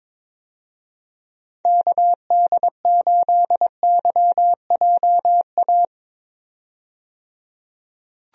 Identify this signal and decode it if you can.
Morse code, heard as a keyed tone, about 22 wpm, 700 Hz. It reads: KD8YJA
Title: USA Call Signs - 6 Characters - 22wpm